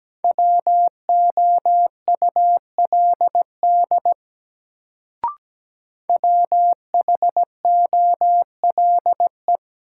Morse code, heard as a keyed tone, about 17 wpm, 700 Hz. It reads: WOULD WHOLE
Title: Sets of 2 Words - New Word Drill - 17wpm